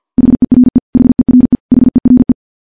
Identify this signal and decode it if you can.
Morse code, heard as a keyed tone, about 50 wpm, 250 Hz. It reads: HERE HERE HERE
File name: morse / morse-training-50WPM